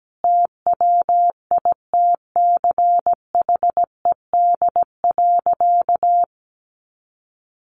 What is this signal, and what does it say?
Morse code, heard as a keyed tone, about 17 wpm, 700 Hz. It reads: TWITCHED.
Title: A Sound Of Thunder 17wpm